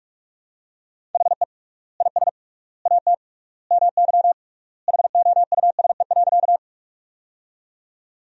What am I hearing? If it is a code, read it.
Morse code, heard as a keyed tone, about 46 wpm, 700 Hz. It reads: HE IS AT MY HOUSE.
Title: 3x Word-Spacing - Sentences from Top 200 Words - 45wpm